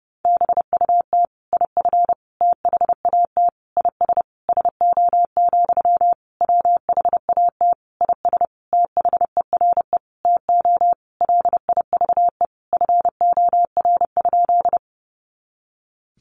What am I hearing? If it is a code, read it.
Morse code, heard as a keyed tone, about 30 wpm, 700 Hz. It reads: BUT IF THAT IS SO, WHAT IS THERE TO LIVE FOR?
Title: Sentences from Top 200 Words 30wpm